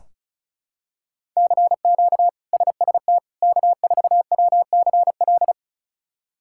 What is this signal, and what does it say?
Morse code, heard as a keyed tone, about 36 wpm, 700 Hz. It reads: CQ SST K4WCL
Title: K1USN SST QSO Practice - 35wpm